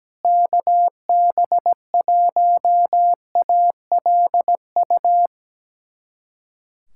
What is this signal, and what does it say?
Morse code, heard as a keyed tone, about 17 wpm, 700 Hz. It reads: KB1ALU
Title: USA Callsign DE Callsign 17wpm